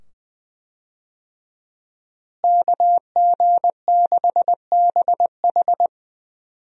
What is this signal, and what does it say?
Morse code, heard as a keyed tone, about 20 wpm, 700 Hz. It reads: KG6BH
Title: USA Call Signs - 5 Characters - Repeated 2x - 20wpm